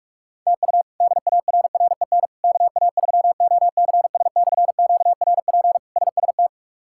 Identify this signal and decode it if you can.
Morse code, heard as a keyed tone, about 47 wpm, 700 Hz. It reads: TU DARREN KA3OCS/QRP SST
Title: K1USN SST QSO Practice - 45wpm